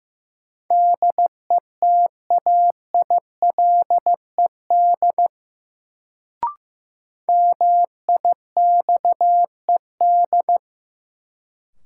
Morse code, heard as a keyed tone, about 15 wpm, 700 Hz. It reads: DETAILED MIXED
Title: Sentences from Top 1500 Words - New Word Drill Alt - 15wpm